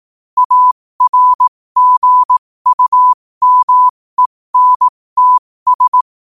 Morse code, heard as a keyed tone, about 18 wpm, spaced as for 15 wpm, 1000 Hz. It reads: ARGUMENTS
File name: qotdfiles / 15wpm